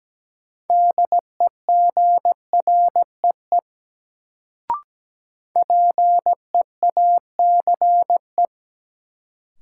Morse code, heard as a keyed tone, about 17 wpm, 700 Hz. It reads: DEGREE PEACE
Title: Sentences from Top 1500 Words - New Word Drill - 17wpm